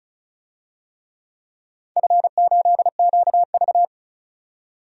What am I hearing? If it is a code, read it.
Morse code, heard as a keyed tone, about 36 wpm, 700 Hz. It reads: F8QV
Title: DXCC-100 Call Signs - 4 Characters - 35wpm